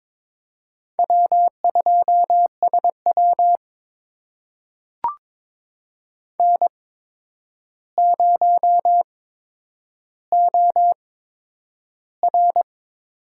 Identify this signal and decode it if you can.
Morse code, heard as a keyed tone, about 22 wpm, spaced as for 9 wpm, 700 Hz. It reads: W2SW N0OR
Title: USA Call Signs - 4 Characters - 8x Character Spacing - 22wpm